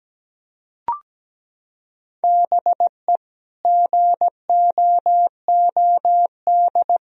Morse code, heard as a keyed tone, about 17 wpm, 700 Hz. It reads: BE GOOD
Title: Sentences from Top 1000 Words - Review Part 2 - 17wpm